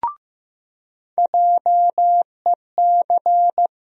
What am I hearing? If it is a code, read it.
Morse code, heard as a keyed tone, about 15 wpm, 700 Hz. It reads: JEC
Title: USA Call Sign Postfixes - 3 Letters 15wpm